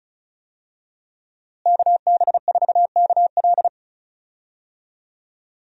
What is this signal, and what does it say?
Morse code, heard as a keyed tone, about 34 wpm, 700 Hz. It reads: KB4KL
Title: USA Call Signs - 5 Characters - Repeated 3x - 35wpm